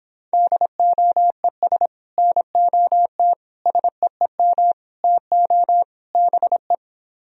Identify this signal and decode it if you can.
Morse code, heard as a keyed tone, about 26 wpm, 700 Hz. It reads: DOES NOT SEEM TO BE